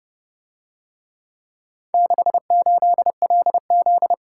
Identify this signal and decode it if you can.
Morse code, heard as a keyed tone, about 28 wpm, 700 Hz. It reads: 68LZ